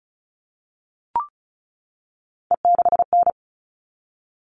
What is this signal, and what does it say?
Morse code, heard as a keyed tone, about 34 wpm, 700 Hz. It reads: E6N
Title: All Cut Numbers - 3-Digit 35wpm